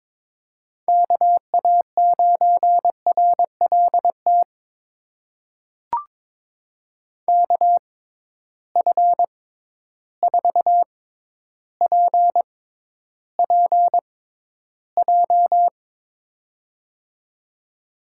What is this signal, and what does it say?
Morse code, heard as a keyed tone, about 22 wpm, spaced as for 11 wpm, 700 Hz. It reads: KA9RLT KF4PPJ
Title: USA Call Signs - 6x Character Spacing - 22wpm